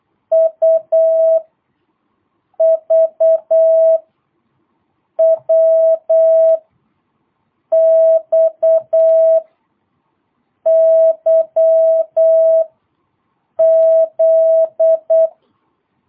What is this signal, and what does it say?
Morse code, heard as a keyed tone, about 8 wpm, spaced as for 5 wpm, 650 Hz. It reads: UVWXYZ
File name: jh8cbh / 2022cw